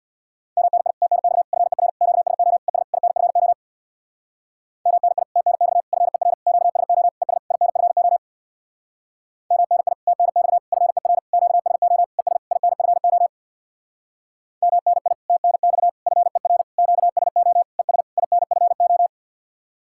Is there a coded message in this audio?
Yes: Morse code from 0.6 to 19.1 s, heard as a keyed tone, about 65 wpm, 700 Hz. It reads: MNI TNX FER QSO ES INFO MNI TNX FER QSO ES INFO MNI TNX FER QSO ES INFO MNI TNX FER QSO ES INFO